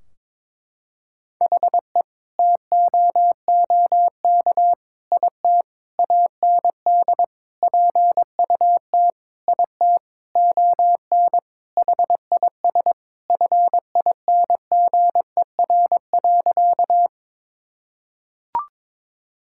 Morse code, heard as a keyed tone, about 22 wpm, 700 Hz. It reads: HE TOOK IT AND PUT IT ON HIS FINGER.
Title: Sentences from Top 1600 Words 22wpm